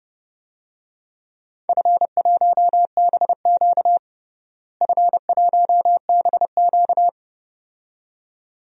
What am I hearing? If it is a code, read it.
Morse code, heard as a keyed tone, about 32 wpm, 700 Hz. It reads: F1BQ F1BQ
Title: DXCC-100 Call Signs - 4 Characters - Repeated 2x - 30wpm